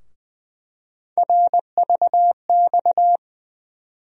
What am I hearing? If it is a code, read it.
Morse code, heard as a keyed tone, about 20 wpm, 700 Hz. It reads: RVX
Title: USA Call Sign Postfixes - 3 Letters 20wpm